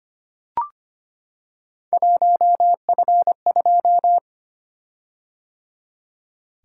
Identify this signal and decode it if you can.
Morse code, heard as a keyed tone, about 25 wpm, 700 Hz. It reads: 1F2
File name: Three Letter-Number Combinations - 25wpm